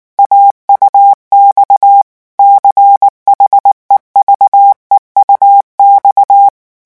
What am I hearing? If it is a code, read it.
Morse code, heard as a keyed tone, about 19 wpm, 800 Hz. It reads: AUX CHEVEUX
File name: morse mp3 / revision9 citations humoristiques vitesse 18 mots minute